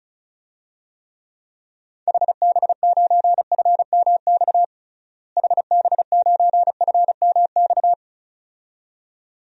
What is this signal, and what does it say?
Morse code, heard as a keyed tone, about 36 wpm, 700 Hz. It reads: HB9FMX HB9FMX
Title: DXCC-100 Call Signs - 6 Characters - Repeated 2x - 35wpm